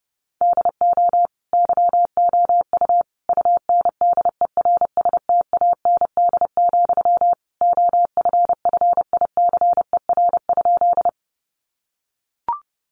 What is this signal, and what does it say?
Morse code, heard as a keyed tone, about 30 wpm, 700 Hz. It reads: DO YOU UNDERSTAND, OFFICER?
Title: Sentences from Top 1200 Words 30wpm